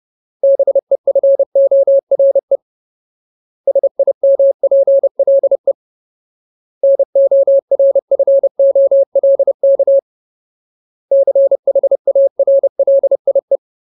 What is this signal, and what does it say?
Morse code, heard as a keyed tone, about 30 wpm, 550 Hz. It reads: BEFORE SIMPLE NORFOLK CHARLIE
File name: Ultimate Word List / 30WPM Ultimate